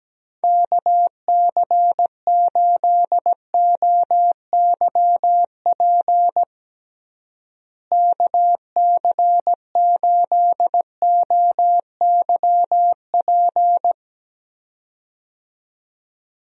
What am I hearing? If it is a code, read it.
Morse code, heard as a keyed tone, about 17 wpm, 700 Hz. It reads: KC8OYP KC8OYP
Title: USA Call Signs - 6 Characters - Repeated 2x - 17wpm